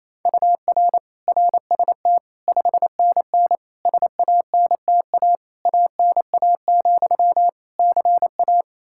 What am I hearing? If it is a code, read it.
Morse code, heard as a keyed tone, about 28 wpm, 700 Hz. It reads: UR RST 5NN SANTA ANA, CA
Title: Practice QSO - Exchange 1 - Beginning - 28wpm